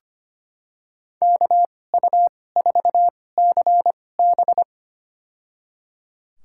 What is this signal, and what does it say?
Morse code, heard as a keyed tone, about 25 wpm, spaced as for 18 wpm, 700 Hz. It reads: KU4CB
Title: USA Call Signs - 5 Characters - 2x Character Spacing - 25wpm